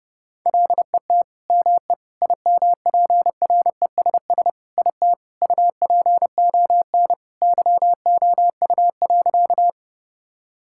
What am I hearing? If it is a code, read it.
Morse code, heard as a keyed tone, about 28 wpm, 700 Hz. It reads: LET ME IMPRESS IT UPON YOU.
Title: Sentences from Top 1600 Words 30wpm